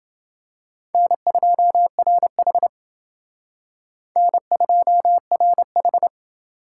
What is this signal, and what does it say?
Morse code, heard as a keyed tone, about 29 wpm, 700 Hz. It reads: N2RH N2RH